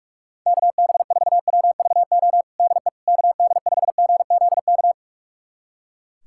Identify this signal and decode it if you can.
Morse code, heard as a keyed tone, about 41 wpm, 700 Hz. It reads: KB4WVO DE KD5GZK
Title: USA Callsign DE Callsign 45wpm